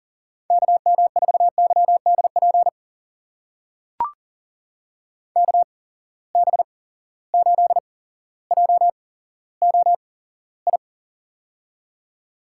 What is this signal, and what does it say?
Morse code, heard as a keyed tone, about 41 wpm, spaced as for 17 wpm, 700 Hz. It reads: KM4YDP KB8JOI
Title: USA Call Signs - 6 Characters - 8x Character Spacing - 40wpm